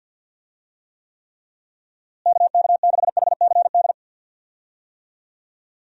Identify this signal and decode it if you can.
Morse code, heard as a keyed tone, about 51 wpm, 700 Hz. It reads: KK6HCD
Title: USA Call Signs - 6 Characters - Repeated 3x - 50wpm